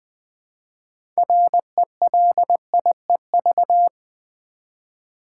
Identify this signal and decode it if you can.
Morse code, heard as a keyed tone, about 20 wpm, 700 Hz. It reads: RELIEV
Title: Build A Word - 2 Letters At A Time - 8-Letter Words - 20wpm